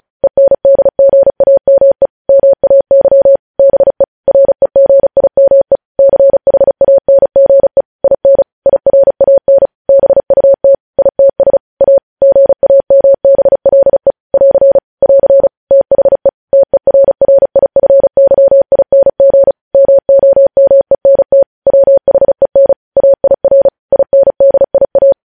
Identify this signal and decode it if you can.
Morse code, heard as a keyed tone, about 36 wpm, 550 Hz. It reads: ENDGAME MAY BE REGIME CHANGE IN IRAN BUT ITS A GAMBLE <AR> <AR> THE TERRIFYING MOMENT WHEN AIR INDIA
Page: news headlines in morse code at 35 wpm